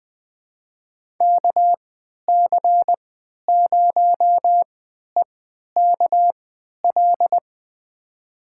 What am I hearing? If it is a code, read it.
Morse code, heard as a keyed tone, about 20 wpm, spaced as for 11 wpm, 700 Hz. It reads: KC0EKL
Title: USA Call Signs - 6 Characters - 3x Character Spacing - 20wpm